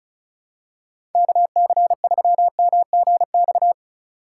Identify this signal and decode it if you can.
Morse code, heard as a keyed tone, about 36 wpm, 700 Hz. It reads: KC3MGX
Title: USA Call Signs - 6 Characters - 4x Character Spacing - 35wpm